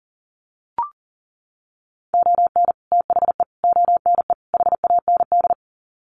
Morse code, heard as a keyed tone, about 40 wpm, 700 Hz. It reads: ON THE ONE HAND
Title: Sets of 4 Words 40wpm